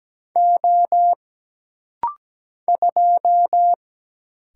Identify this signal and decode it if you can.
Morse code, heard as a keyed tone, about 17 wpm, 700 Hz. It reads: O 2
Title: Single Letter-Number Alt - Rapid-Fire 17wpm